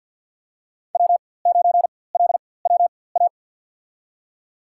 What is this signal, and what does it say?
Morse code, heard as a keyed tone, about 51 wpm, spaced as for 23 wpm, 700 Hz. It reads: W9LWA
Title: USA Call Signs - 5 Characters - 4x Character Spacing - 50wpm